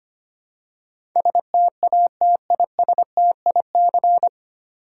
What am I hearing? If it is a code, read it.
Morse code, heard as a keyed tone, about 25 wpm, 700 Hz. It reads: STATISTIC
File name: practice / 25wpm / 9 Letter Words - First 3 Letters Given - 25wpm